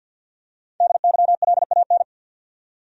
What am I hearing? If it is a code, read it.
Morse code, heard as a keyed tone, about 50 wpm, 700 Hz. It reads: DYLAN
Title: USA Common Names - 50wpm